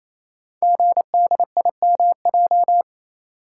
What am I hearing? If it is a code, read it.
Morse code, heard as a keyed tone, about 29 wpm, 700 Hz. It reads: GDIMJ
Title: Code Groups - 5 Letters - 28wpm